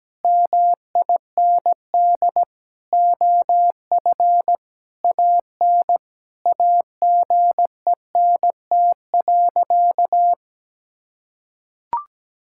Morse code, heard as a keyed tone, about 17 wpm, 700 Hz. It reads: MIND OF AN AGENT.
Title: Sentences from Top 1600 Words 17wpm